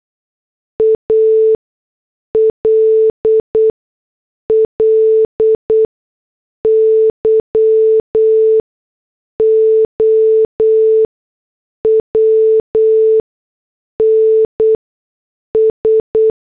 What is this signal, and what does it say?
Morse code, heard as a keyed tone, about 8 wpm, spaced as for 6 wpm, 440 Hz. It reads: ALLY OWNS
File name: morse / headlines